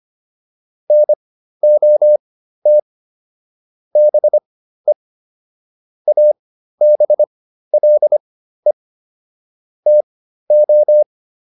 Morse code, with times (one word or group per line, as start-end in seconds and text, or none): 0.9-2.8 NOT
4.0-4.9 BE
6.1-8.7 ABLE
9.9-11.0 TO